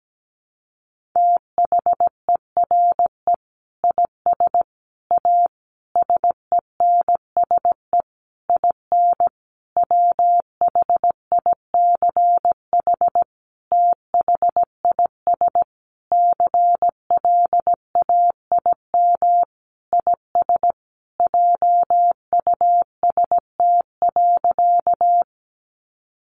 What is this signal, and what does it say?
Morse code, heard as a keyed tone, about 17 wpm, 700 Hz. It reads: THERE IS A SENSE IN WHICH THIS CLAIM IS JUST.
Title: Sentences from Top 1500 Words 17wpm